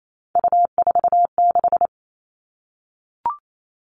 Morse code, heard as a keyed tone, about 28 wpm, 700 Hz. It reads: U46
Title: Three Challenging Letter-Number Combinations 28wpm